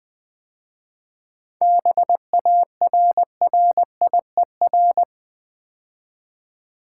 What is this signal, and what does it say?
Morse code, heard as a keyed tone, about 20 wpm, 700 Hz. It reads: BARRIER
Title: 7 Letter Words - First 3 Letters Given - 20wpm